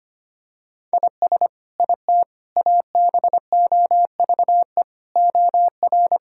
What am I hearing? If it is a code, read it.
Morse code, heard as a keyed tone, about 25 wpm, 700 Hz. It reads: IS IT ABOVE OR
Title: Sentences from Top 1000 Words - Review Part 5 - 25wpm